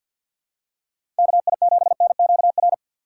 Morse code, heard as a keyed tone, about 53 wpm, 700 Hz. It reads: K I 7 N Q R
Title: USA Call Signs - 6 Characters - 8x Character Spacing - 50wpm